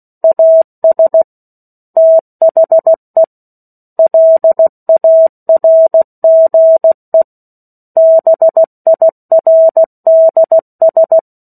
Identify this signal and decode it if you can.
Morse code, heard as a keyed tone, about 16 wpm, 650 Hz. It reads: AS THE LARGE BIRDS
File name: 03-Chapter-03 16 WPM